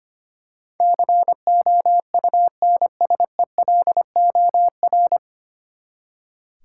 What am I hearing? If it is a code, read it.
Morse code, heard as a keyed tone, about 25 wpm, 700 Hz. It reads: COUNSELOR
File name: Nine Letter Words - 25wpm